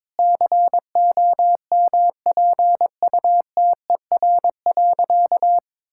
Morse code, heard as a keyed tone, about 22 wpm, 690 Hz. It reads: COMPUTER.
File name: practice / 22wpm / Easy Sentences with Hints - 22wpm